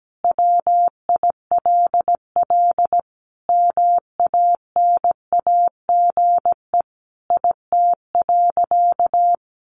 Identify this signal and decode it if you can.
Morse code, heard as a keyed tone, about 17 wpm, 700 Hz. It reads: WILL MANAGE IT.
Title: Sentences from Top 1400 Words 17wpm